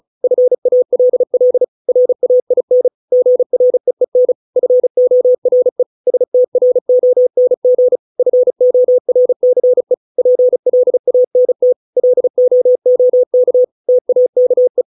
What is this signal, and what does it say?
Morse code, heard as a keyed tone, about 34 wpm, 500 Hz. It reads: FALL RAIN GREEN FORE STRONG FORCE PLANT LOOK TAKE